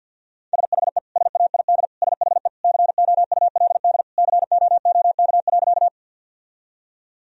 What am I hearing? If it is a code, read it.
Morse code, heard as a keyed tone, about 50 wpm, 700 Hz. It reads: SHE SAID SHE COULD COOK.
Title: Sentences from Top 1000 Words - Review Part 4 - 50wpm